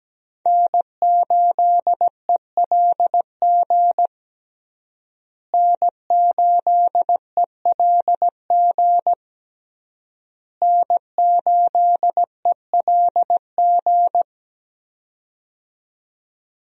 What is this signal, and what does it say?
Morse code, heard as a keyed tone, about 17 wpm, 700 Hz. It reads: N8ELG N8ELG N8ELG
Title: USA Call Signs - 5 Characters - Repeated 3x - 17wpm